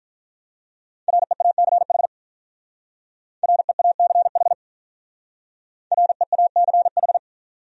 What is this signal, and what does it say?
Morse code, heard as a keyed tone, about 44 wpm, 700 Hz. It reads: REACH REACH REACH